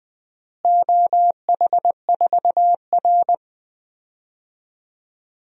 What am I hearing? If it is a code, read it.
Morse code, heard as a keyed tone, about 20 wpm, 700 Hz. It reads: OH4R
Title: DXCC-100 Call Signs - 4 Characters - Repeated 3x - 20wpm